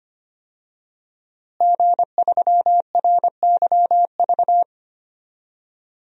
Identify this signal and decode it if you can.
Morse code, heard as a keyed tone, about 25 wpm, 700 Hz. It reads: G3RYV
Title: DXCC-100 Call Signs - 5 Characters - Repeated 2x - 25wpm